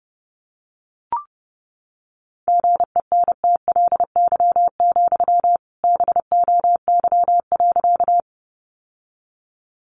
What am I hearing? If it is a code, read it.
Morse code, heard as a keyed tone, about 30 wpm, 700 Hz. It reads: GENTLY, BOY.
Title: Sentences from Top 1500 Words 30wpm